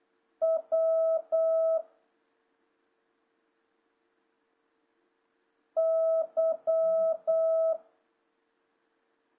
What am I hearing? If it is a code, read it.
Morse code, heard as a keyed tone, about 8 wpm, 650 Hz. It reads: W Y
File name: jh8cbh / 2026cw